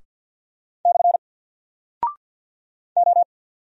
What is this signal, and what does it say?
Morse code, heard as a keyed tone, about 48 wpm, 700 Hz. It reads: / O
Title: Single Character (Letter:Number:Punctuation:Prosign) Alt 50wpm